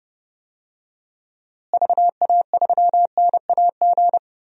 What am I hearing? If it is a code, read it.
Morse code, heard as a keyed tone, about 32 wpm, 700 Hz. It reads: VA3NAG
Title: DXCC-100 Call Signs - 6 Characters - Repeated 3x - 30wpm